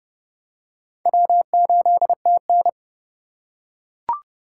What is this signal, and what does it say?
Morse code, heard as a keyed tone, about 30 wpm, 700 Hz. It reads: W8TN